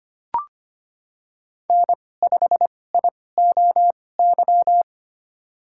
Morse code, heard as a keyed tone, about 25 wpm, spaced as for 18 wpm, 700 Hz. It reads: N5IOY
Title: USA Call Signs - 5 Characters - 2x Character Spacing - 25wpm